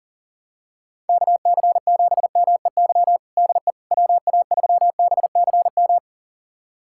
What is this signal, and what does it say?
Morse code, heard as a keyed tone, about 40 wpm, 700 Hz. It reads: KC7MEY DE WA3BCM
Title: USA Callsign DE Callsign 40wpm